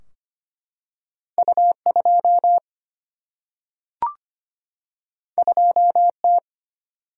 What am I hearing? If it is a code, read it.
Morse code, heard as a keyed tone, about 25 wpm, 700 Hz. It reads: U2 2T
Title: Single-Letter Single-Number 25wpm